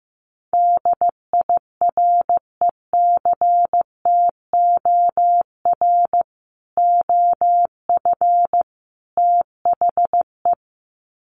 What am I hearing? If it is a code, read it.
Morse code, heard as a keyed tone, about 15 wpm, 700 Hz. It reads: DIRECTOR OF THE
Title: Sets of 3 Words 15wpm